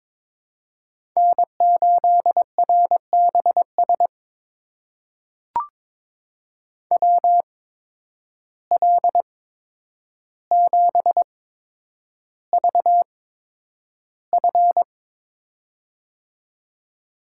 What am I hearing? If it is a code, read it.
Morse code, heard as a keyed tone, about 22 wpm, spaced as for 9 wpm, 700 Hz. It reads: N8RBS WL7VF
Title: USA Call Signs - 5 Characters - 8x Character Spacing - 22wpm